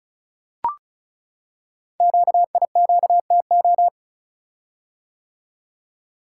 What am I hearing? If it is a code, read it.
Morse code, heard as a keyed tone, about 36 wpm, 690 Hz. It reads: QIQTO
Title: Code Groups - 5 Letters - 35wpm